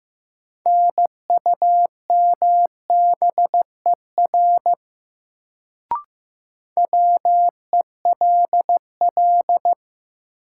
Morse code, heard as a keyed tone, about 15 wpm, 700 Hz. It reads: NUMBER WELL
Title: Sets of 2 Words - New Word Drill - 15wpm